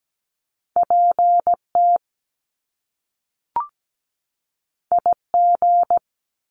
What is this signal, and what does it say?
Morse code, heard as a keyed tone, about 17 wpm, 700 Hz. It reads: PT IG
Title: Top Two Letter Combinations in Words 17wpm